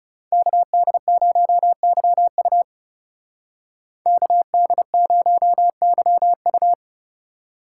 Morse code, heard as a keyed tone, about 36 wpm, 700 Hz. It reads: KD0YU KD0YU
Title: Speed Racing - USA 2x2 Call Signs 30wpm